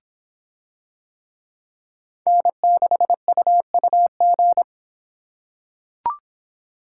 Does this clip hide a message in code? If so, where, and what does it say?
Yes: Morse code from 2.3 to 4.6 s, heard as a keyed tone, about 26 wpm, 700 Hz. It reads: N6UUG